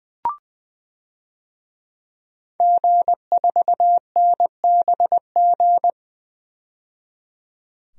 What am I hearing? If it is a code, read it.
Morse code, heard as a keyed tone, about 20 wpm, 700 Hz. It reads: G4NBG
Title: DXCC-100 Call Signs - 5 Characters - 20wpm